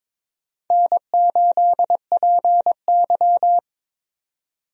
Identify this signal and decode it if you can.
Morse code, heard as a keyed tone, about 22 wpm, 700 Hz. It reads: N8PY
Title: USA Call Signs - 4 Characters - 22wpm